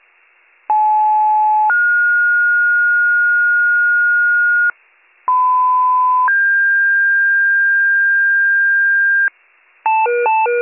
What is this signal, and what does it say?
Two-tone page, heard as a one-second tone, then a three-second tone, about 860 then 1430 Hz, about 1010 then 1600 Hz, with noise. It is heard from 0.7 to 9.3 s.